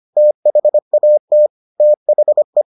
Morse code, heard as a keyed tone, about 25 wpm, 600 Hz. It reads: THAT THE